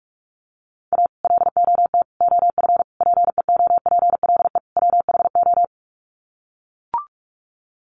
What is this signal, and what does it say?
Morse code, heard as a keyed tone, about 44 wpm, 700 Hz. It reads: A LOT OF PEOPLE WHO
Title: Sets of 5 Words 45wpm